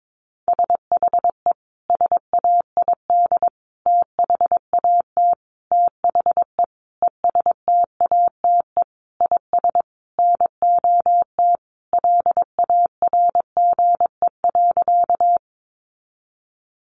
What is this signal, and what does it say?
Morse code, heard as a keyed tone, about 22 wpm, 700 Hz. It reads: SHE SAID THAT THE ESTATE IS NOT LARGE.
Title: Sentences from Top 1200 Words 22wpm